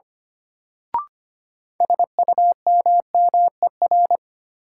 Morse code, heard as a keyed tone, about 25 wpm, 700 Hz. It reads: SUMMER
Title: Sentences from Top 1100 Words - New Word Drill Alt - 25wpm